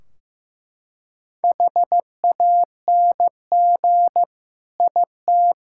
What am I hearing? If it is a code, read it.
Morse code, heard as a keyed tone, about 15 wpm, 700 Hz. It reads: HANG IT
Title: Sentences from Top 1600 Words 15wpm